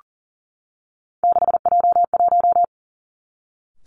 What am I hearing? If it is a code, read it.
Morse code, heard as a keyed tone, about 38 wpm, 700 Hz. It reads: 6J1
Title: Three Challenging Letter-Number Combinations 40wpm